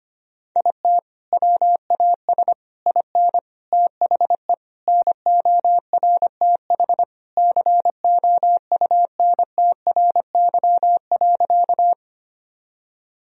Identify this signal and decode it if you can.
Morse code, heard as a keyed tone, about 25 wpm, 700 Hz. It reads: IT WAS IN THE NORTH COUNTRY.
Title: Sentences from Top 1000 Words - Review Part 1 - 25wpm